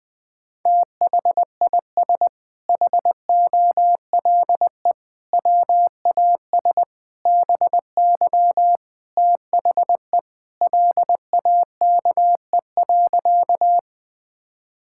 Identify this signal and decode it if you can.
Morse code, heard as a keyed tone, about 20 wpm, 700 Hz. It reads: THIS HOLE WAS BY THE LAKE.
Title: Sentences from Top 1600 Words 20wpm